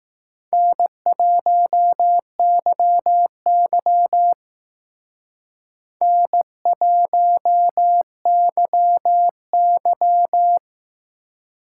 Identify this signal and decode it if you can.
Morse code, heard as a keyed tone, about 18 wpm, 700 Hz. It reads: N1YY N1YY
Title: Speed Racing - USA 1x2 Call Signs 15wpm